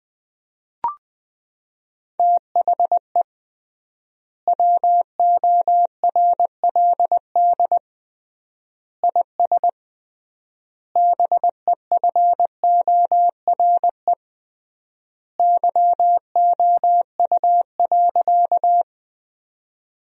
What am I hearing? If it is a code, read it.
Morse code, heard as a keyed tone, about 20 wpm, 700 Hz. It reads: THE WORLD IS BEFORE YOU.
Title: 3x Word-Spacing - Sentences from Top 200 Words - 20wpm